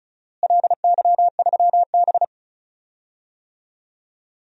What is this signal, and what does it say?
Morse code, heard as a keyed tone, about 36 wpm, 690 Hz. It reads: LY3B
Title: DXCC-100 Call Signs - 4 Characters - Repeated 3x - 35wpm